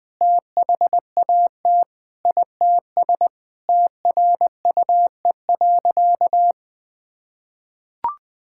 Morse code, heard as a keyed tone, about 21 wpm, 700 Hz. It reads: THAT ITS TRUE.
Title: Sentences with 7-Word Phrases - Spoken Beginning - 20wpm